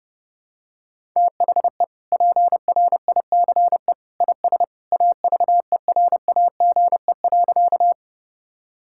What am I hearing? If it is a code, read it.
Morse code, heard as a keyed tone, about 32 wpm, 700 Hz. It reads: THE PRICE IS AVERAGE.
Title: Sentences from Top 1000 Words - Review Part 6 - 30wpm